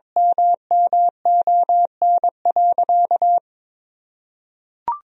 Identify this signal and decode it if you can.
Morse code, heard as a keyed tone, about 21 wpm, 700 Hz. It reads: MMON.
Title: Sentences from Top 1000 Words - Review Part 3 - 22wpm